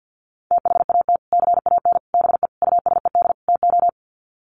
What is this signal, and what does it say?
Morse code, heard as a keyed tone, about 50 wpm, 700 Hz. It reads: THAT CAN BE USED TO
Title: Sets of 5 Words Encore 50wpm